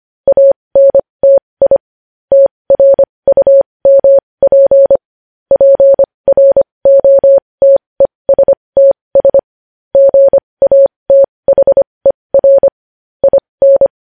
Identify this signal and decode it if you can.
Morse code, heard as a keyed tone, about 25 wpm, spaced as for 20 wpm, 550 Hz. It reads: ANTI TRUMP PROTESTS GATHER IN
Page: News Headlines in Morse Code at 20 WPM